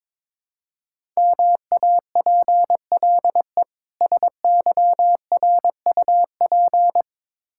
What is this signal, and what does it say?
Morse code, heard as a keyed tone, about 22 wpm, 700 Hz. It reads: MAPLE SYRUP
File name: Difficult Words with Hints - 22wpm